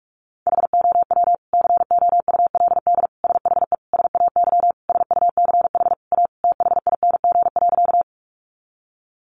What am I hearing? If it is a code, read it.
Morse code, heard as a keyed tone, about 44 wpm, 700 Hz. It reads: HOW COULD SHE SAY SUCH A THING.
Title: Sentences from Top 200 Words 45wpm